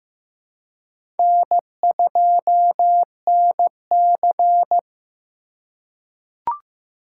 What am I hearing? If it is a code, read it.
Morse code, heard as a keyed tone, about 16 wpm, 700 Hz. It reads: N2NC
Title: K1USN SST QSO Practice - 15wpm